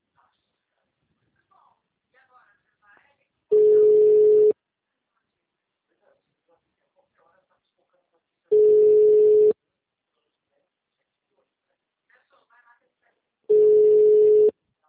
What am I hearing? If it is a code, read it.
Ringback tone (Europe, 425 Hz), with speech.